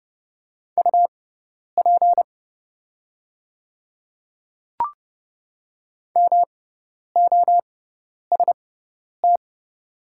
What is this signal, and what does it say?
Morse code, heard as a keyed tone, about 30 wpm, spaced as for 10 wpm, 700 Hz. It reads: UP MOST
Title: 6x Character Spacing - Top 100 Words - 30wpm